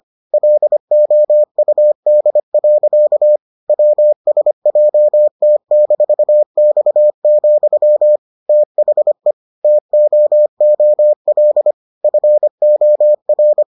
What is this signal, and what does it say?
Morse code, heard as a keyed tone, about 25 wpm, 600 Hz. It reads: LOUD. WSJT-X, THE TOOL FOR